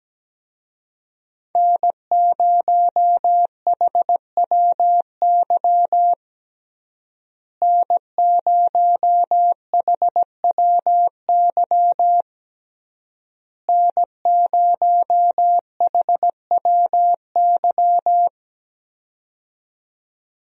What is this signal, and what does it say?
Morse code, heard as a keyed tone, about 17 wpm, 700 Hz. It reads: N0HWY N0HWY N0HWY
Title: USA Call Signs - 5 Characters - Repeated 3x - 17wpm